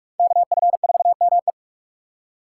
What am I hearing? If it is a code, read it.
Morse code, heard as a keyed tone, about 41 wpm, 700 Hz. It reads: KF4ME